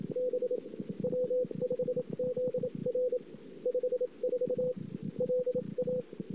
Morse code, heard as a keyed tone, about 27 wpm, spaced as for 23 wpm, 500 Hz. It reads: B W5ZR 54 LA
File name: CW / 2008-ss-cw-k5tr-op-n5rz -22:00